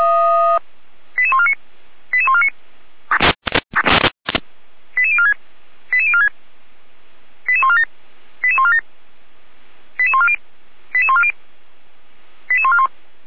Five-tone selective call (ZVEI), heard as quick runs of five tones, tones of about 70 ms. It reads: 88149 88149 88046 88046 88147 88147 88140 88140 88141